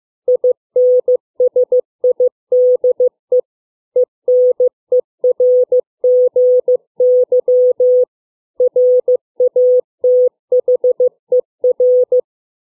Morse code, heard as a keyed tone, about 15 wpm, 500 Hz. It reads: INSIDE ENERGY RATHER
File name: CW / fixed length words 2